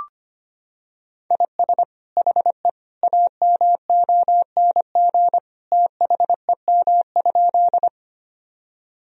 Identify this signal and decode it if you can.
Morse code, heard as a keyed tone, about 25 wpm, 700 Hz. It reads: IS HE AMONG THEM?
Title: Sentences from Top 1000 Words - Review Part 3 - 25wpm